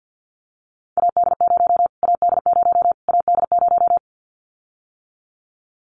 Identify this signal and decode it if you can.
Morse code, heard as a keyed tone, about 48 wpm, 700 Hz. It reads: AD0 AD0 AD0
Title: Callsign Prefixes for Top 25 DXCC - Repeated 3x - 50wpm